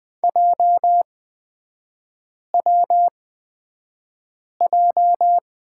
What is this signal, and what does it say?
Morse code, heard as a keyed tone, about 20 wpm, spaced as for 5 wpm, 700 Hz. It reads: JWJ
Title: 243 Lesson 39e - Letter z - Three Characters 20wpm